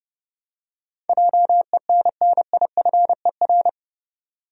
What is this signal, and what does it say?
Morse code, heard as a keyed tone, about 30 wpm, 700 Hz. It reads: JENNIFER
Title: USA Common Names - 30wpm